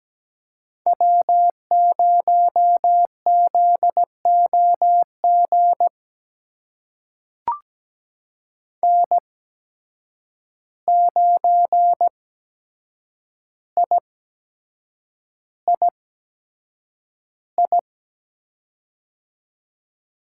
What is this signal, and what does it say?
Morse code, heard as a keyed tone, about 17 wpm, spaced as for 7 wpm, 700 Hz. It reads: W0ZOG N9III
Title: USA Call Signs - 5 Characters - 8x Character Spacing - 17wpm